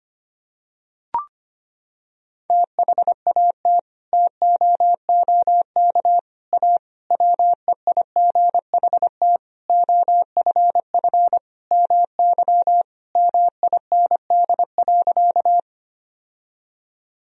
Morse code, heard as a keyed tone, about 25 wpm, 700 Hz. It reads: THAT TOOK A WEIGHT OFF MY MIND.
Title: Sentences from Top 1000 Words - Review Part 3 - 25wpm